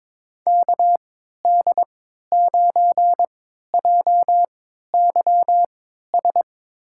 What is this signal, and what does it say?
Morse code, heard as a keyed tone, about 22 wpm, spaced as for 12 wpm, 700 Hz. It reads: KD9JYS
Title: USA Call Signs - 6 Characters - 3x Character Spacing - 22wpm